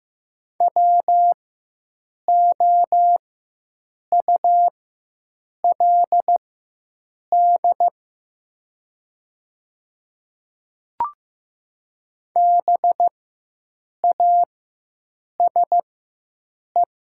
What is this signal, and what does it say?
Morse code, heard as a keyed tone, about 15 wpm, spaced as for 7 wpm, 700 Hz. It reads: WOULD BASE
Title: 4x Character Spacing - Top 500 Words - 15wpm